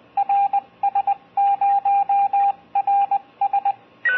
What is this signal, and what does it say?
Morse code, heard as a keyed tone, about 20 wpm, 800 Hz, with radio noise. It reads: RS0RS
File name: ARISS SSTV / SuitSat-SSTV